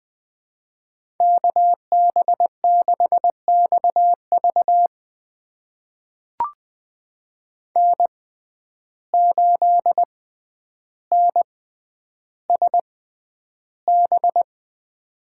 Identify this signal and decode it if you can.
Morse code, heard as a keyed tone, about 20 wpm, spaced as for 10 wpm, 700 Hz. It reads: KB6XV N8NSB
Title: USA Call Signs - 5 Characters - 6x Character Spacing - 20wpm